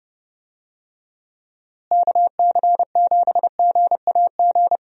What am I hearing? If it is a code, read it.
Morse code, heard as a keyed tone, about 31 wpm, 700 Hz. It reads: KC7GAG
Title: USA Call Signs - 6 Characters - Repeated 2x - 30wpm